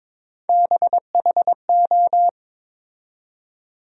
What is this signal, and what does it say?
Morse code, heard as a keyed tone, about 22 wpm, 700 Hz. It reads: BHO
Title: USA Call Sign Postfixes - 3 Letters 22wpm